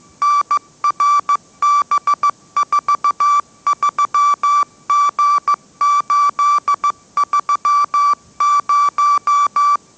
Morse code, heard as a keyed tone, about 15 wpm, 1160 Hz. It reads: NRB43G830
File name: mrx / mrx066 16wpm